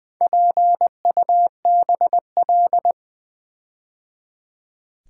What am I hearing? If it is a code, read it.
Morse code, heard as a keyed tone, about 20 wpm, 700 Hz. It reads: PUBL
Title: Build A Word - 4 Letters At A Time - 11-Letter Words - 20wpm